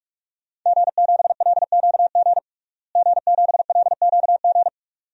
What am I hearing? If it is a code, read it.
Morse code, heard as a keyed tone, about 47 wpm, 700 Hz. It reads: G7LQG G7LQG
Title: DXCC-100 Call Signs - 5 Characters - Repeated 2x - 45wpm